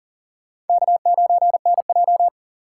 Morse code, heard as a keyed tone, about 52 wpm, 700 Hz. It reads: K9NJ